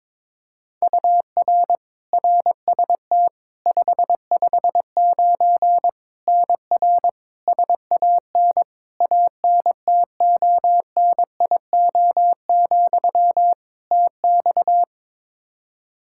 Morse code, heard as a keyed tone, about 22 wpm, 700 Hz. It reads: UR RST 559 NR SAN ANTONIO, TX
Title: K1USN SST QSO Practice 22wpm